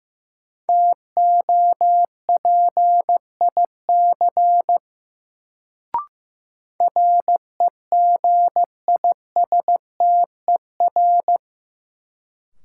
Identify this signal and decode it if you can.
Morse code, heard as a keyed tone, about 15 wpm, 700 Hz. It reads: TOPIC REGISTER
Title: Sentences from Top 1200 Words - New Word Drill - 15wpm